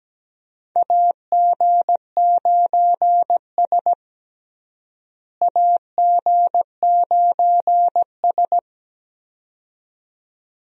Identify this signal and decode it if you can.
Morse code, heard as a keyed tone, about 17 wpm, 700 Hz. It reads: AG9S AG9S
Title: USA Call Signs - 4 Characters - Repeated 2x - 17wpm